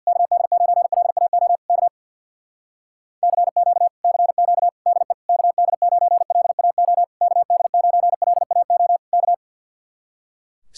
Morse code, heard as a keyed tone, about 58 wpm, 700 Hz. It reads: KD9LAO K CQ CQ DE KD9LAO KD9LAO K
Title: Speed-Racing Common QSO Phrases - 50wpm